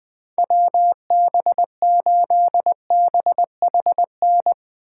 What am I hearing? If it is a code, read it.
Morse code, heard as a keyed tone, about 21 wpm, 700 Hz. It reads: WB8BHN
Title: 2023 SOTA Chaser Call Signs - 20wpm